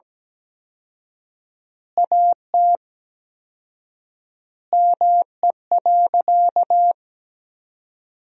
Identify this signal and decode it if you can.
Morse code, heard as a keyed tone, about 17 wpm, 700 Hz. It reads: AT ME.
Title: 4x Word-Spacing - Sentences from Top 200 Words - 17wpm